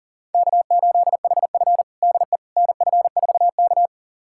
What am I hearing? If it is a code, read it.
Morse code, heard as a keyed tone, about 37 wpm, 700 Hz. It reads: K8HF DE NF4K
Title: USA Callsign DE Callsign - 4 Characters - 40wpm